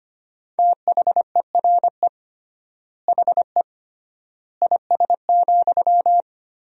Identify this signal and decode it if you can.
Morse code, heard as a keyed tone, about 25 wpm, 700 Hz. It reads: THERE HE IS,
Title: 3x Word-Spacing - Sentences from Top 100 Words - 25wpm